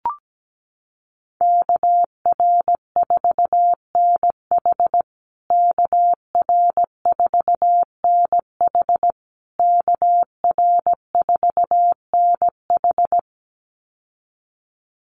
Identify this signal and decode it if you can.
Morse code, heard as a keyed tone, about 17 wpm, 700 Hz. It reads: KR4NH KR4NH KR4NH
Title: USA Call Signs Repeated 3x 17wpm